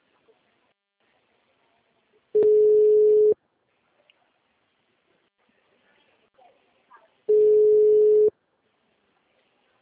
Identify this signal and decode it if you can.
Ringback tone (Europe, 425 Hz), with speech.